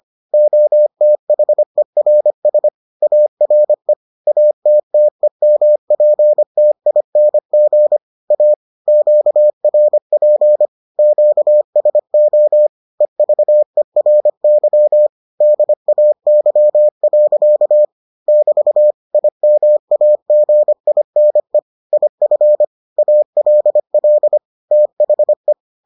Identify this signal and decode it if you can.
Morse code, heard as a keyed tone, about 25 wpm, 600 Hz. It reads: OTHERS ARE ATTEMPTING A QRP QSO EVERY DAY. <BT> IMAGINE IF ALL THE